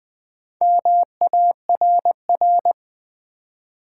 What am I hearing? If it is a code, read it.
Morse code, heard as a keyed tone, about 20 wpm, 700 Hz. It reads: MARR
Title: Build A Word - 2 Letters At A Time - 8-Letter Words - 20wpm